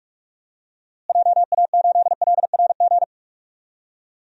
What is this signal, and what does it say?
Morse code, heard as a keyed tone, about 46 wpm, 700 Hz. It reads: JA8LRG
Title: DXCC-100 Call Signs - 6 Characters - Repeated 3x - 45wpm